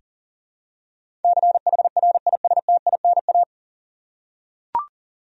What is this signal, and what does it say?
Morse code, heard as a keyed tone, about 40 wpm, 700 Hz. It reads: CHRISTINA
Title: USA Common Names - 40wpm